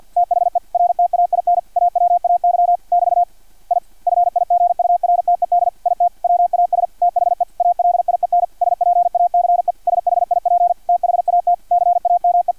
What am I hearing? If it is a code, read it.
Morse code, heard as a keyed tone, about 50 wpm, 700 Hz. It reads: THE RTAIN AWAY <BT> I FIGURTED IT WAS THE ALIEN SPACE SHIP THAT CAME
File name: archived / CRSnet